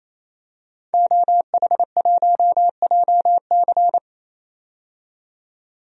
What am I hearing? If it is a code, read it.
Morse code, heard as a keyed tone, about 28 wpm, 700 Hz. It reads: OH1JC
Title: DXCC-100 Call Signs - 5 Characters - 28wpm